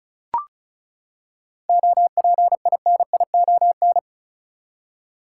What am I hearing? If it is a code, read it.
Morse code, heard as a keyed tone, about 34 wpm, 700 Hz. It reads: OPINION
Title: Seven Letter Words - 35wpm